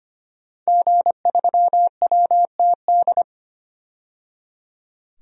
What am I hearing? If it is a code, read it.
Morse code, heard as a keyed tone, about 25 wpm, 700 Hz. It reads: G3WTD
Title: DXCC-100 Call Signs - 5 Characters - 25wpm